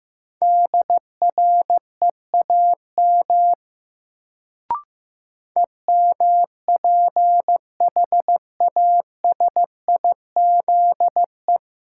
Morse code, heard as a keyed tone, about 16 wpm, 700 Hz. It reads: DREAM EMPHASIZE
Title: Sentences from Top 1500 Words - New Word Drill - 15wpm